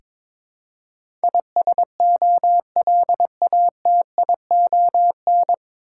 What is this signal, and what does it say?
Morse code, heard as a keyed tone, about 21 wpm, 700 Hz. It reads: ISOLATION
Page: Morse Code Ninja practice 22wpm